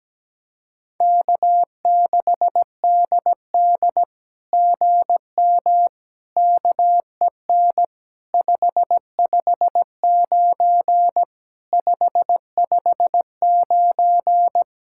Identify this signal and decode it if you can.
Morse code, heard as a keyed tone, about 17 wpm, 700 Hz. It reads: K6DD GM KEN 559 559
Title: SOTA QSO Practice 17wpm